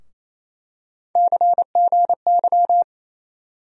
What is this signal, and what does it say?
Morse code, heard as a keyed tone, about 28 wpm, 700 Hz. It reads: CGY
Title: USA Call Sign Postfixes - 3 Letters 28wpm